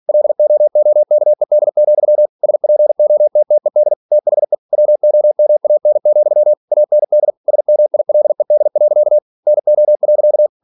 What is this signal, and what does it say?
Morse code, heard as a keyed tone, about 46 wpm, 600 Hz. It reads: LOOKED, SPOTTED THE WOMAN, AND SMILED. NO.